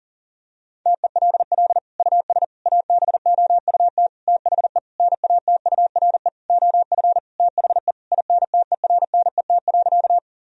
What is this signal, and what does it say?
Morse code, heard as a keyed tone, about 40 wpm, 700 Hz. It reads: TELL US ABOUT THE NATURE OF THE INTERNET.
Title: Sentences from Top 1000 Words - Review Part 4 - 40wpm